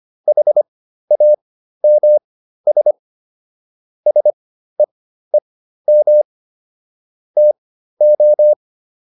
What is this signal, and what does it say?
Morse code, heard as a keyed tone, about 25 wpm, spaced as for 13 wpm, 600 Hz. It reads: HAMS SEEM TO